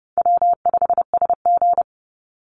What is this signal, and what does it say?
Morse code, heard as a keyed tone, about 30 wpm, 700 Hz. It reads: W5SG
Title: Call Signs - 30wpm